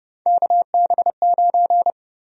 Morse code, heard as a keyed tone, about 30 wpm, 700 Hz. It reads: KB9
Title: USA Call Sign Prefixes - 30wpm